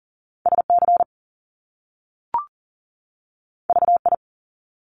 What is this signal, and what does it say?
Morse code, heard as a keyed tone, about 41 wpm, 700 Hz. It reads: SC VI
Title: Top Two Letter Combinations in Words 40wpm